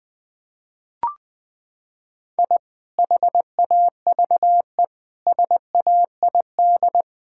Morse code, heard as a keyed tone, about 20 wpm, 700 Hz. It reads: I HAVE SAID
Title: Sentences from Top 1000 Words - Review Part 6 - 20wpm